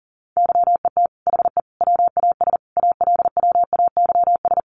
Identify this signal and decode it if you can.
Morse code, heard as a keyed tone, about 40 wpm, 700 Hz. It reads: YET HE WAS ALWAYS
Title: Sentences from Top 1300 Words 40wpm